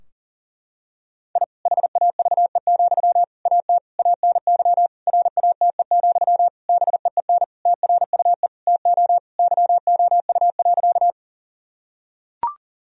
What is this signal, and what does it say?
Morse code, heard as a keyed tone, about 40 wpm, 700 Hz. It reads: I HAVE, AT ANY RATE, BEEN TRUE TO YOU.
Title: Sentences from Top 1000 Words - Review Part 4 - 40wpm